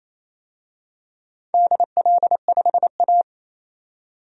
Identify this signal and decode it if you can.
Morse code, heard as a keyed tone, about 27 wpm, 700 Hz. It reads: DL5A
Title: DXCC-100 Call Signs - 4 Characters - 28wpm